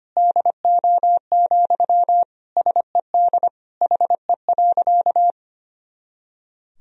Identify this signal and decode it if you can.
Morse code, heard as a keyed tone, about 25 wpm, 700 Hz. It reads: DO, SED HE.
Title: Sentences from Top 100 Words and Abbreviations 25wpm